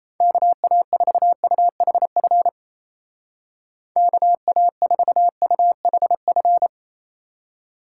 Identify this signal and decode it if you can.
Morse code, heard as a keyed tone, about 35 wpm, 700 Hz. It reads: KA4UHF KA4UHF